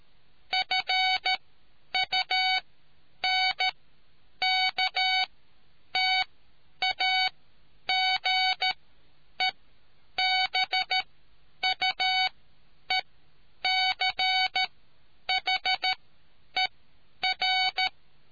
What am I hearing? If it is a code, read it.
Morse code, heard as a keyed tone, about 13 wpm, spaced as for 9 wpm, 750 Hz. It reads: FUNKTAGEBUECHER